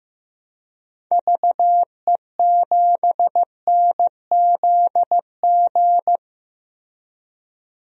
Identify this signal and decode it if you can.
Morse code, heard as a keyed tone, about 16 wpm, 700 Hz. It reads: VE7NZG
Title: DXCC-100 Call Signs - 6 Characters - 15wpm